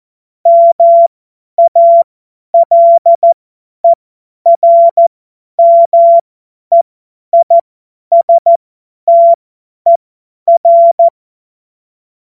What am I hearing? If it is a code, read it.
Morse code, heard as a keyed tone, about 14 wpm, spaced as for 10 wpm, 680 Hz. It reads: MALERMEISTER
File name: Morsekurs / Witze / Witz 23 Januar 2024 50 F70